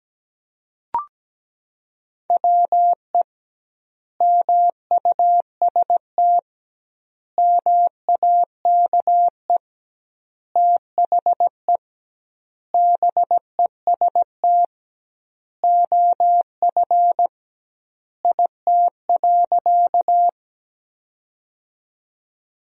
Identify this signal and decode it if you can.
Morse code, heard as a keyed tone, about 17 wpm, 700 Hz. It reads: WE MUST MAKE THE BEST OF IT.
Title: 2x Word-Spacing - Sentences from Top 400 Words - 17wpm